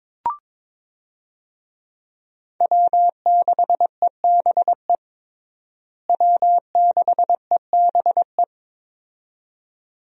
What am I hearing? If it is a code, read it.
Morse code, heard as a keyed tone, about 22 wpm, 700 Hz. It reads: W6EBE W6EBE
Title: USA Call Signs - 5 Characters - Repeated 2x - 22wpm